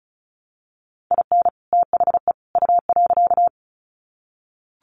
Morse code, heard as a keyed tone, about 34 wpm, 700 Hz. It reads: IN THE U.
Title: Sets of 3 Words 35wpm